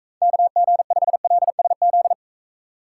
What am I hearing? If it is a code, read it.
Morse code, heard as a keyed tone, about 36 wpm, 700 Hz. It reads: KG5LSZ